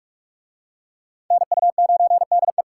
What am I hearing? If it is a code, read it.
Morse code, heard as a keyed tone, about 40 wpm, 700 Hz. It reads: NU9DE